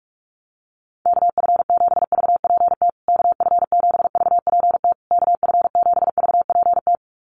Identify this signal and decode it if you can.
Morse code, heard as a keyed tone, about 44 wpm, 690 Hz. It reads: KF7VPT KF7VPT KF7VPT
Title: USA Call Signs Repeated 3x 45wpm